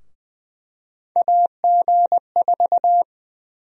Morse code, heard as a keyed tone, about 20 wpm, 700 Hz. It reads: AG4
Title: USA Call Sign Prefixes - Repeated 2x - 20wpm